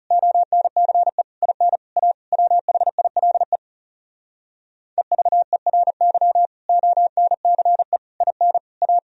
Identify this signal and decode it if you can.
Morse code, heard as a keyed tone, about 37 wpm, 700 Hz. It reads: ONCE IN A WHILE EVERY ONCE IN A
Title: Speed-Racing - Sets of 5 Words 30wpm